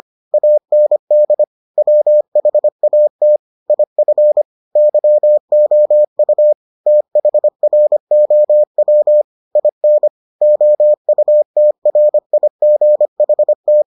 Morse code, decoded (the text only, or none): AND WHAT IF YOU THROW IN OUTRIGHT